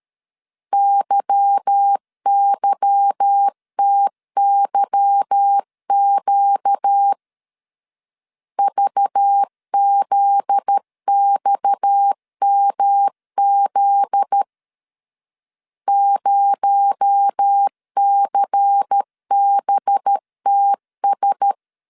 Morse code, heard as a keyed tone, about 13 wpm, 800 Hz. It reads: YYTYQ VZXMZ 0CBTS
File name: LEZIONI / 8 YJ45 / 8 Y 12wpm 2228